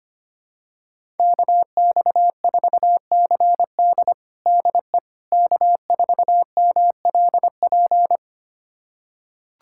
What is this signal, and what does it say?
Morse code, heard as a keyed tone, about 25 wpm, 700 Hz. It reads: KX4CD DE K4MLP
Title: USA Callsign DE Callsign - 5 Characters - 25wpm